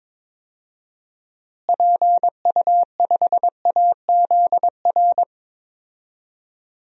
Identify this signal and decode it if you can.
Morse code, heard as a keyed tone, about 21 wpm, 700 Hz. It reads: PU5AZR
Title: DXCC-100 Call Signs - 6 Characters - Repeated 3x - 22wpm